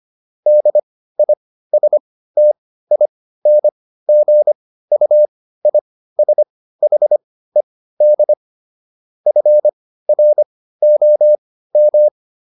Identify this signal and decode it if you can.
Morse code, heard as a keyed tone, about 25 wpm, spaced as for 15 wpm, 600 Hz. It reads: DISTINGUISHED FROM